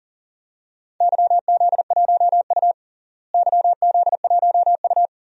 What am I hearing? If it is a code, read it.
Morse code, heard as a keyed tone, about 40 wpm, 700 Hz. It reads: YZ1U YZ1U
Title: DXCC-100 Call Signs - 4 Characters - Repeated 2x - 40wpm